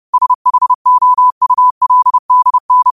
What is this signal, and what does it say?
Morse code, heard as a keyed tone, about 30 wpm, 1000 Hz. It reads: SHOULDN